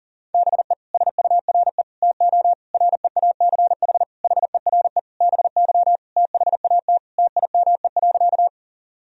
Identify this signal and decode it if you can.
Morse code, heard as a keyed tone, about 40 wpm, 700 Hz. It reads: BE SURE TO REACH HERE BY THAT TIME.
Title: Sentences from Top 1000 Words - Review Part 2 - 40wpm